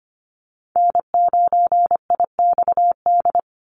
Morse code, heard as a keyed tone, about 25 wpm, 700 Hz. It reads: N9IXD
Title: Call Signs Repeated 2x - 25wpm